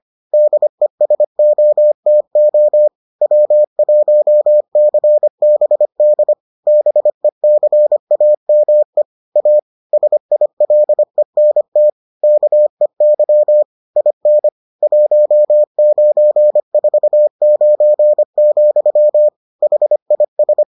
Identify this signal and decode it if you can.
Morse code, heard as a keyed tone, about 25 wpm, 600 Hz. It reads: DESOTO W1CBD BECAME A SILENT KEY IN 1949, HIS